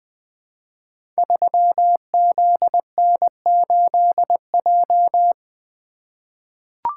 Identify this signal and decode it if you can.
Morse code, heard as a keyed tone, about 20 wpm, 700 Hz. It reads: 3ZN8J
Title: Code Groups - 5 Letter:Numbers - 20wpm